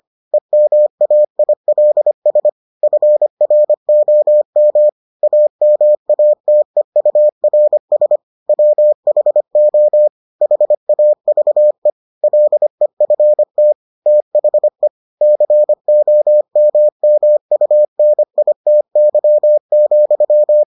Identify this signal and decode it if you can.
Morse code, heard as a keyed tone, about 25 wpm, 600 Hz. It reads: EMAILS FROM AMATEURS WHO HAVE LEFT THE COMMUNITY,